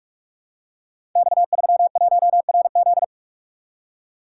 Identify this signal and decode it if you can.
Morse code, heard as a keyed tone, about 44 wpm, 690 Hz. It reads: X31RZ
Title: Code Groups - 5 Letter:Numbers - 45wpm